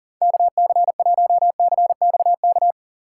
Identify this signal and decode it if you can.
Morse code, heard as a keyed tone, about 40 wpm, 700 Hz. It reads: KC1CXK
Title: USA Call Signs - 4x Character Spacing - 40wpm